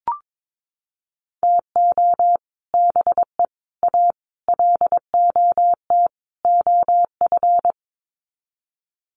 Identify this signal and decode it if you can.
Morse code, heard as a keyed tone, about 22 wpm, 700 Hz. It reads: TO BE A LOT OF
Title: Sets of 5 Words Encore 22wpm